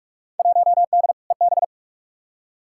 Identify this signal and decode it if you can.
Morse code, heard as a keyed tone, about 45 wpm, 700 Hz. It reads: 1D EB
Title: Field Day Exchanges - 45wpm